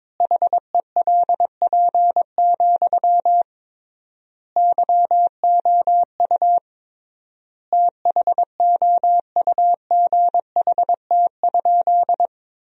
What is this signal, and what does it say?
Morse code, heard as a keyed tone, about 22 wpm, 700 Hz. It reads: HELP, YOU THOUGHT?
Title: 3x Word-Spacing - Sentences from Top 300 Words - 22wpm